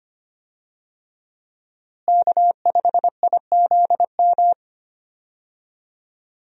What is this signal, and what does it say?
Morse code, heard as a keyed tone, about 25 wpm, 700 Hz. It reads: K5IZM
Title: USA Call Signs - 5 Characters - Repeated 3x - 25wpm